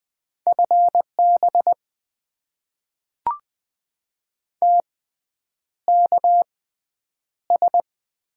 Morse code, heard as a keyed tone, about 20 wpm, spaced as for 9 wpm, 700 Hz. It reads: FB TKS
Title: Top 90 QSO Elements - 6x Character Spacing - 20wpm